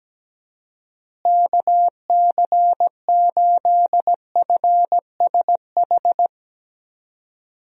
Morse code, heard as a keyed tone, about 17 wpm, 700 Hz. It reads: KC8FSH
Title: USA Call Signs - 6 Characters - Repeated 2x - 17wpm